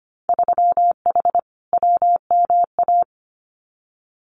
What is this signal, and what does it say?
Morse code, heard as a keyed tone, about 25 wpm, 690 Hz. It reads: 3H WMA
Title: Winter Field Day Exchanges Encore 25wpm